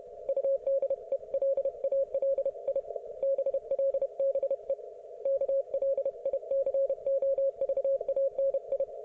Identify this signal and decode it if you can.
Morse code, heard as a keyed tone, about 31 wpm, 530 Hz. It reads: UDELALI BLBE KLICOVANI